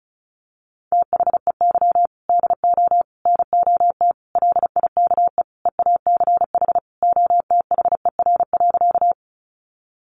Morse code, heard as a keyed tone, about 34 wpm, 700 Hz. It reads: THEY DO NOT LIKE EACH OTHER.
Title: Easy Phrases 35wpm